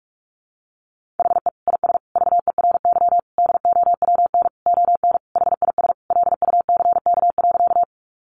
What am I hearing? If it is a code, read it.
Morse code, heard as a keyed tone, about 44 wpm, 700 Hz. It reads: HE IS VERY DOWN ON HIS LUCK.
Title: Sentences from Top 1500 Words 45wpm